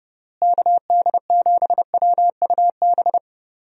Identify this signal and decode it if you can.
Morse code, heard as a keyed tone, about 30 wpm, 700 Hz. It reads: KD7WUB
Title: USA Call Signs - 6 Characters - Repeated 3x - 30wpm